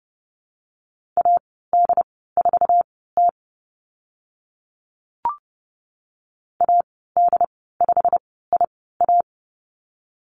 Morse code, heard as a keyed tone, about 30 wpm, spaced as for 17 wpm, 700 Hz. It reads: AD4T AD5IA
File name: Lesson 15g - Number 5 - Call Signs 30wpm